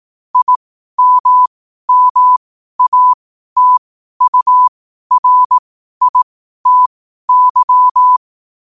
Morse code, heard as a keyed tone, about 18 wpm, spaced as for 12 wpm, 1000 Hz. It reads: IMMATURITY
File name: qotdfiles / 12wpm